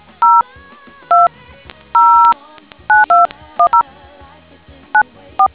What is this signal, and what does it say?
DTMF, heard as telephone touch tones, tones of about 170 ms. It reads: 0209210 94